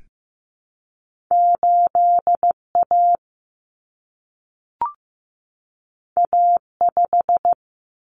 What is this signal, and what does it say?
Morse code, heard as a keyed tone, about 15 wpm, 700 Hz. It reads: 8A A5
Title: Common Cut Numbers - 2-Digit 15wpm